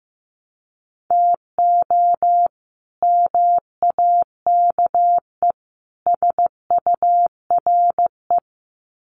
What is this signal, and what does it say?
Morse code, heard as a keyed tone, about 15 wpm, 700 Hz. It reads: TO MAKE SURE
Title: Sets of 3 Words 15wpm